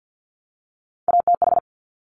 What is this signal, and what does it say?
Morse code, heard as a keyed tone, about 50 wpm, 700 Hz. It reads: ATH
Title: Top Three Letter Combinations in Words 50wpm